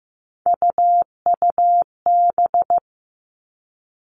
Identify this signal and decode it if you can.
Morse code, heard as a keyed tone, about 15 wpm, 700 Hz. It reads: UUB
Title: Three Challenging Letter-Number Combinations 15wpm